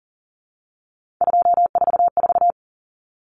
Morse code, heard as a keyed tone, about 40 wpm, 700 Hz. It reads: 244
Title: Numbers - 3-Digit 40wpm